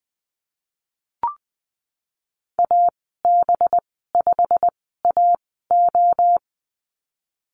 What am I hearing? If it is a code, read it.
Morse code, heard as a keyed tone, about 20 wpm, spaced as for 15 wpm, 700 Hz. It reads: AB5AO
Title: 193 Lesson 31g - Letter v - Call Signs 20wpm